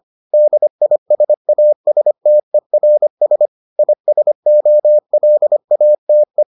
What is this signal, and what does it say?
Morse code, heard as a keyed tone, about 25 wpm, 600 Hz. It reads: DISASTERS ISOLATE